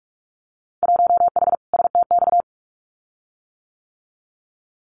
Morse code, heard as a keyed tone, about 46 wpm, 700 Hz. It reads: 1H STX
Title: Winter Field Day Exchanges 45wpm